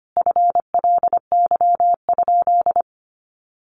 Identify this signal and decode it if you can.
Morse code, heard as a keyed tone, about 25 wpm, 700 Hz. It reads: FLY?
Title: Sentences from Top 500 Words 25wpm